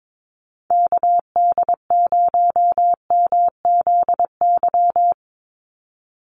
Morse code, heard as a keyed tone, about 22 wpm, 700 Hz. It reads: KD0MZY
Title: USA Call Signs 22wpm